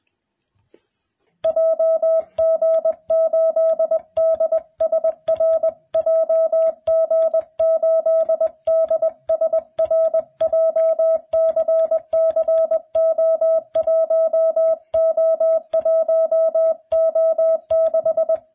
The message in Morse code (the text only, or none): JG8DSR JG8DSR JCCO1O1O6